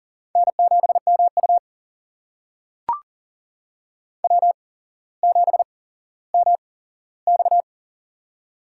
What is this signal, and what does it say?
Morse code, heard as a keyed tone, about 40 wpm, spaced as for 17 wpm, 700 Hz. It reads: N7MU W7MX
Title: USA Call Signs - 4 Characters - 8x Character Spacing - 40wpm